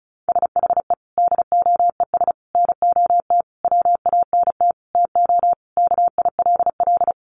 Morse code, heard as a keyed tone, about 34 wpm, 700 Hz. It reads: SHE DOES NOT WANT TO KILL
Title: Additional Easy Phrases 35wpm